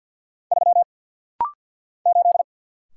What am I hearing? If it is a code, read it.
Morse code, heard as a keyed tone, about 50 wpm, 700 Hz. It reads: 3 8
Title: Single Letter-Number - Rapid-Fire 50wpm